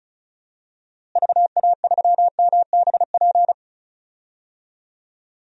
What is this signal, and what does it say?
Morse code, heard as a keyed tone, about 34 wpm, 700 Hz. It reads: VA3MBP
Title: DXCC-100 Call Signs - 6 Characters - Repeated 2x - 35wpm